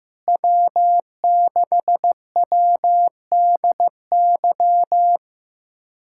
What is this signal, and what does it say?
Morse code, heard as a keyed tone, about 15 wpm, 700 Hz. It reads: W6WDY
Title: USA Callsign DE Callsign - 5 Characters - 15wpm